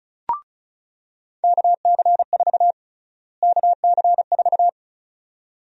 Morse code, heard as a keyed tone, about 36 wpm, 700 Hz. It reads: KC4 KC4
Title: USA Call Sign Prefixes - Repeated 2x - 35wpm